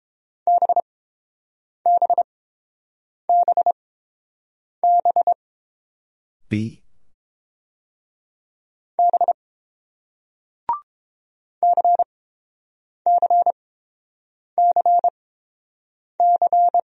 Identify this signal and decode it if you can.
Morse code, heard as a keyed tone, about 31 wpm, spaced as for 7 wpm, 690 Hz. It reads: BBBB B CCCC